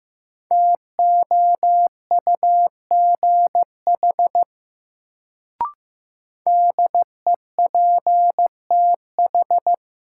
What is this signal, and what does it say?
Morse code, heard as a keyed tone, about 15 wpm, 700 Hz. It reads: TOUGH DEPTH
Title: Sentences from Top 1200 Words - New Word Drill Alt - 15wpm